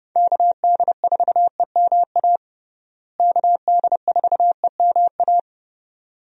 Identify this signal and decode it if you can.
Morse code, heard as a keyed tone, about 30 wpm, 700 Hz. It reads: KD4EMA KD4EMA
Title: USA Call Signs - 6 Characters - Repeated 2x - 30wpm